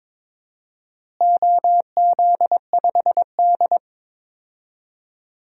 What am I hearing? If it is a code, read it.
Morse code, heard as a keyed tone, about 22 wpm, 700 Hz. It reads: OZ5D
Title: DXCC-100 Call Signs - 4 Characters - Repeated 2x - 22wpm